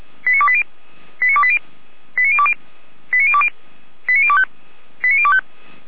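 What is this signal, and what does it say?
Five-tone selective call (ZVEI), heard as quick runs of five tones, tones of about 70 ms. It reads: 89280 89280 89020 89020 89025 89025
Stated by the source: ZVEI five-tone call received on SDR.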